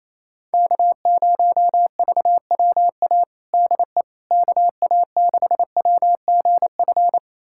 Morse code, heard as a keyed tone, about 28 wpm, 700 Hz. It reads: K0VWA DE KA6WGF
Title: USA Callsign DE Callsign 28wpm